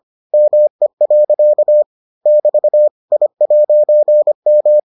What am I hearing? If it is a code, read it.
Morse code, heard as a keyed tone, about 25 wpm, 600 Hz. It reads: ME. <BT> I'M